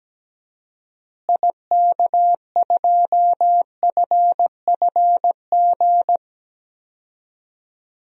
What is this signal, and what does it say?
Morse code, heard as a keyed tone, about 17 wpm, 700 Hz. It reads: IK2FFG
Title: DXCC-100 Call Signs - 6 Characters - Repeated 2x - 17wpm